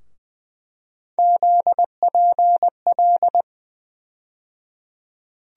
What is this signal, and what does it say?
Morse code, heard as a keyed tone, about 20 wpm, 700 Hz. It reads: ZPL
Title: USA Call Sign Postfixes - 3 Letters 20wpm